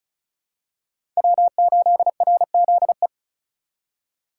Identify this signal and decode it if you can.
Morse code, heard as a keyed tone, about 34 wpm, 700 Hz. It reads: W8RZE
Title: USA Call Signs - 5 Characters - 3x Character Spacing - 35wpm